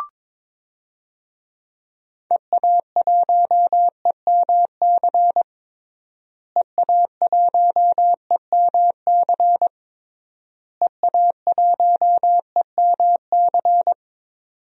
Morse code, heard as a keyed tone, about 22 wpm, 700 Hz. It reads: EA1EMC EA1EMC EA1EMC
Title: DXCC-100 Call Signs - 6 Characters - Repeated 3x - 22wpm